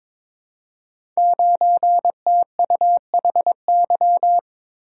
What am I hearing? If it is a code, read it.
Morse code, heard as a keyed tone, about 22 wpm, 700 Hz. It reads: 9TUHY
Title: Code Groups - 5 Letter:Numbers - 22wpm